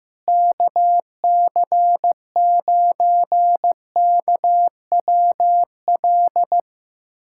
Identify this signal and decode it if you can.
Morse code, heard as a keyed tone, about 15 wpm, 700 Hz. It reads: KC9KWL
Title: USA Callsign DE Callsign - 6 Characters - 15wpm